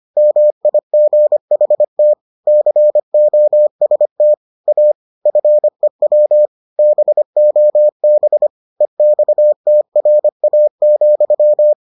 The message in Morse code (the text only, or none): MIGHT COST A FEW BOB EXTRA,